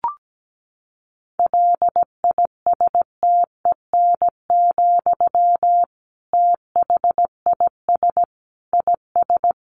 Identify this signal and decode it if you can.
Morse code, heard as a keyed tone, about 17 wpm, 700 Hz. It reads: LISTEN, THIS IS
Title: Sentences from Top 1100 Words 17wpm